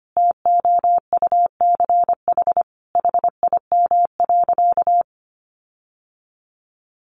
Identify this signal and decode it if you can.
Morse code, heard as a keyed tone, about 25 wpm, 700 Hz. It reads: TOUCH HIM.
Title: Sentences from Top 1300 Words 25wpm